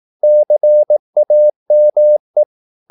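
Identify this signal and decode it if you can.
Morse code, heard as a keyed tone, about 18 wpm, 600 Hz. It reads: CAME
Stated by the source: Morse practice at 18 wpm, English text, from CW med Gnister.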